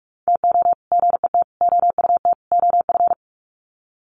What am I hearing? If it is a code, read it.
Morse code, heard as a keyed tone, about 44 wpm, 700 Hz. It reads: TO GET OUT OF
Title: Sets of 4 Words 45wpm